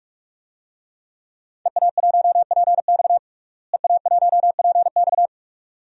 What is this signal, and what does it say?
Morse code, heard as a keyed tone, about 42 wpm, 700 Hz. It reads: EA1PX EA1PX
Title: DXCC-100 Call Signs - 5 Characters - Repeated 2x - 45wpm